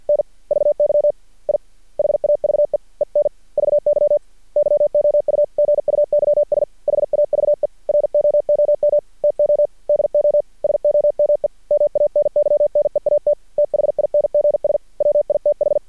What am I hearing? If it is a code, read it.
Morse code, heard as a keyed tone, about 48 wpm, 590 Hz. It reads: N VY I HAVE ENVY YOU GUYS HAVE ROOM TO DO SOME MANY NEAT THINGS WITH
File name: archived / CRSnet